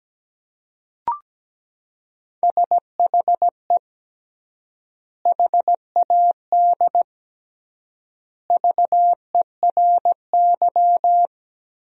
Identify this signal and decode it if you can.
Morse code, heard as a keyed tone, about 17 wpm, 700 Hz. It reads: SHE HAD VERY
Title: 3x Word-Spacing - Sentences from Top 400 Words - 17wpm